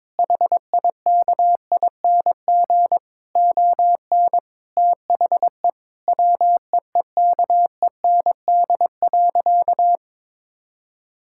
Morse code, heard as a keyed tone, about 21 wpm, 700 Hz. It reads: HIKING ON THE WEEKEND.
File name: Difficult Sentences with Hints - 22wpm